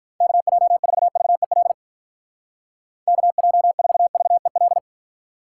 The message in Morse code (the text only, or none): KJ4VEL KJ4VEL